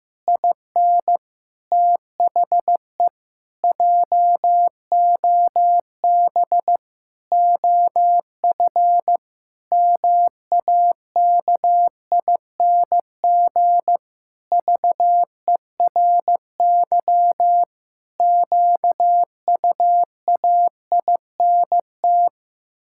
Morse code, heard as a keyed tone, about 14 wpm, 700 Hz. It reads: IN THE JOB OF MAKING VERY QUAINT
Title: Pangram Challenge v2 - 15wpm